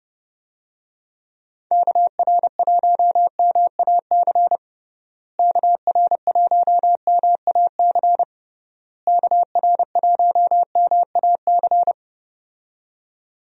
Morse code, heard as a keyed tone, about 32 wpm, 700 Hz. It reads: KR1MAC KR1MAC KR1MAC
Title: USA Call Signs - 6 Characters - Repeated 3x - 30wpm